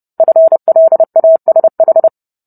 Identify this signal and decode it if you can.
Morse code, heard as a keyed tone, about 30 wpm, 650 Hz. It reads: FLASH